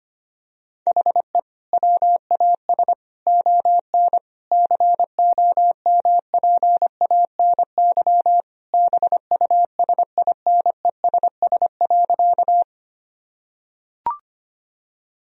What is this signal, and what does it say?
Morse code, heard as a keyed tone, about 25 wpm, 700 Hz. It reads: HE WAS ON COMPANY BUSINESS.
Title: Sentences from Top 1000 Words - Review Part 3 - 25wpm